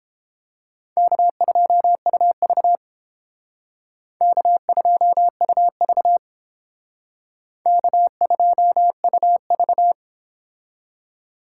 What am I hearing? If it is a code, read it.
Morse code, heard as a keyed tone, about 29 wpm, 700 Hz. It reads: K2UV K2UV K2UV